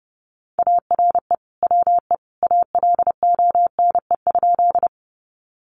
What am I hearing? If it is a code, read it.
Morse code, heard as a keyed tone, about 30 wpm, 700 Hz. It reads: ARE WE ALONE?
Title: Sentences from Top 1000 Words 30wpm